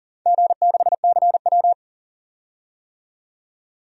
Morse code, heard as a keyed tone, about 41 wpm, 700 Hz. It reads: G6CW
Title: DXCC-100 Call Signs - 4 Characters - 40wpm